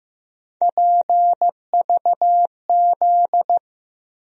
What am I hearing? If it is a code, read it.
Morse code, heard as a keyed tone, about 15 wpm, 700 Hz. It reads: PVZ
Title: USA Call Sign Postfixes - 3 Letters 15wpm